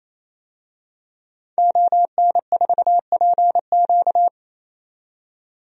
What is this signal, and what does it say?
Morse code, heard as a keyed tone, about 27 wpm, 700 Hz. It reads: ON4PQ
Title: DXCC-100 Call Signs - 5 Characters - 28wpm